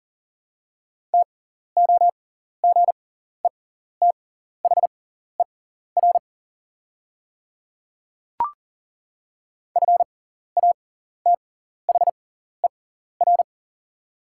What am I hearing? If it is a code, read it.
Morse code, heard as a keyed tone, about 40 wpm, spaced as for 14 wpm, 700 Hz. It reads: TOGETHER FATHER
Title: 6x Character Spacing - Top 300 Words - 40wpm